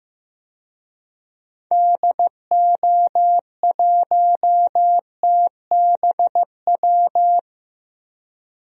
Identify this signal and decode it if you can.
Morse code, heard as a keyed tone, about 15 wpm, 700 Hz. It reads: DO1TBW
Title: DXCC-100 Call Signs - 6 Characters - Repeated 2x - 15wpm